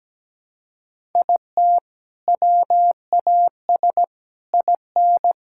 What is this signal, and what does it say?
Morse code, heard as a keyed tone, about 17 wpm, 700 Hz. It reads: IT WAS IN
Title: Sentences from Top 1000 Words - Review Part 5 - 17wpm